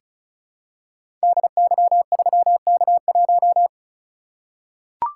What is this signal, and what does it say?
Morse code, heard as a keyed tone, about 36 wpm, 700 Hz. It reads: DY3K1
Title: Code Groups - 5 Letter:Numbers - 35wpm